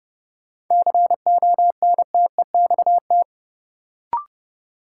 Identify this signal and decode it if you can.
Morse code, heard as a keyed tone, about 29 wpm, 700 Hz. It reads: CONTEXT
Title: Sentences from Top 1000 Words - Review Part 7 - New Word Drill - 30wpm